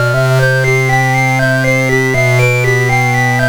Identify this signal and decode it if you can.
There is no code.